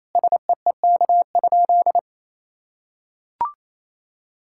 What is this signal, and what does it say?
Morse code, heard as a keyed tone, about 28 wpm, 700 Hz. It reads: SEEK?
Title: Sentences from Top 1600 Words 28wpm